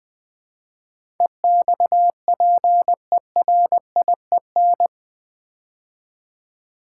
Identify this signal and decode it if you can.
Morse code, heard as a keyed tone, about 20 wpm, 700 Hz. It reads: EXPERIEN
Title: Build A Word - 4 Letters At A Time - 10-Letter Words - 20wpm